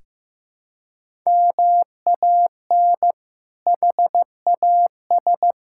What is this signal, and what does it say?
Morse code, heard as a keyed tone, about 15 wpm, 700 Hz. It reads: MAN HAS
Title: Sentences from Top 1000 Words - Review Part 6 - 15wpm